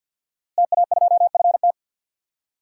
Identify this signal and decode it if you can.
Morse code, heard as a keyed tone, about 49 wpm, 700 Hz. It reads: TA2FT
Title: DXCC-100 Call Signs - 5 Characters - Repeated 2x - 50wpm